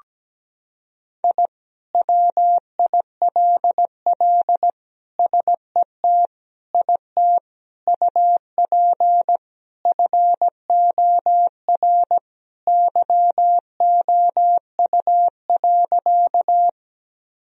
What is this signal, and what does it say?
Morse code, heard as a keyed tone, about 17 wpm, 700 Hz. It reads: I WILL SET IT UP FOR YOU.
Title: Sentences from Top 1000 Words - Review Part 1 - 17wpm